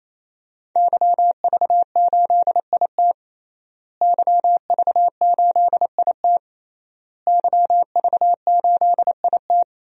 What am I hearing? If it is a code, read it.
Morse code, heard as a keyed tone, about 28 wpm, 700 Hz. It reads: YV8IT YV8IT YV8IT
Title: DXCC-100 Call Signs - 5 Characters - Repeated 3x - 28wpm